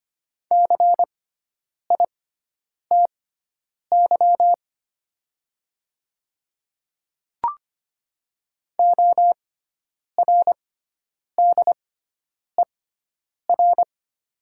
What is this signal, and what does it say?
Morse code, heard as a keyed tone, about 25 wpm, spaced as for 9 wpm, 700 Hz. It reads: CITY ORDER
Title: 6x Character Spacing - Top 500 Words - 25wpm